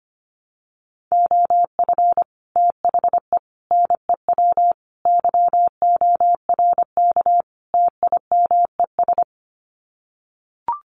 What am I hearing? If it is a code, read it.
Morse code, heard as a keyed tone, about 25 wpm, 700 Hz. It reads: OF THE NEW YORK TIMES
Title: Sets of 5 Words 25wpm